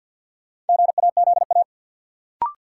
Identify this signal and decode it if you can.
Morse code, heard as a keyed tone, about 53 wpm, 700 Hz. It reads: GAZA
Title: Sets of 5 Words - New Word Drill - 50wpm